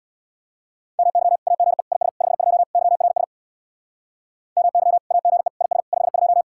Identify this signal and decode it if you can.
Morse code, heard as a keyed tone, about 61 wpm, 700 Hz. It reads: MY AGE IS 52 YRS MY AGE IS 52
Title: Speed-Racing Common QSO Phrases - 50wpm